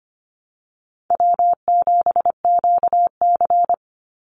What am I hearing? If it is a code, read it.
Morse code, heard as a keyed tone, about 25 wpm, 700 Hz. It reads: W7QC
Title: Call Signs Repeated 2x - 25wpm